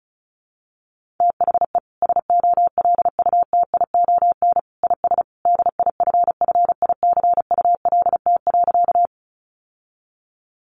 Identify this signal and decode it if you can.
Morse code, heard as a keyed tone, about 34 wpm, 700 Hz. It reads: THE SOLUTION IS DIFFICULT.
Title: Sentences from Top 1300 Words 35wpm